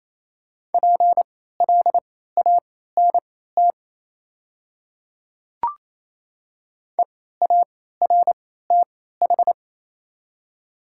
Morse code, heard as a keyed tone, about 28 wpm, spaced as for 16 wpm, 690 Hz. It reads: PLANT EARTH
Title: 3x Character Spacing - Top 300 Words - 28wpm